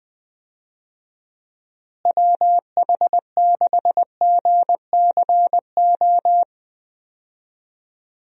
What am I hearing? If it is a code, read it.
Morse code, heard as a keyed tone, about 20 wpm, 700 Hz. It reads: WH6GCO
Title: USA Call Signs - 6 Characters - Repeated 3x - 20wpm